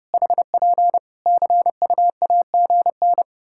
Morse code, heard as a keyed tone, about 30 wpm, 700 Hz. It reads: HP CUAGN
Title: Practice QSO - Exchange 4 - Ending - 30wpm